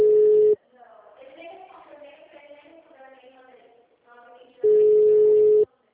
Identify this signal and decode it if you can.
Ringback tone (Europe, 425 Hz), with speech.
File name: out- -2007- - - .292